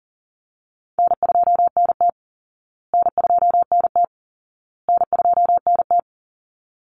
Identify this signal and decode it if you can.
Morse code, heard as a keyed tone, about 40 wpm, 700 Hz. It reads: N2NT N2NT N2NT
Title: Call Signs Repeated 2x - 40wpm